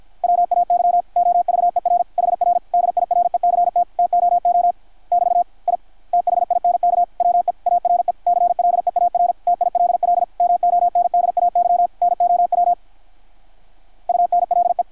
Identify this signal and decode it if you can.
Morse code, heard as a keyed tone, about 53 wpm, 720 Hz, with radio noise. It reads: WAY OVER HR DIRECT TOO <BT> I THINK WE ARE CLEAR TILL MONDAY NOW UNLE
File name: archived / CRSnet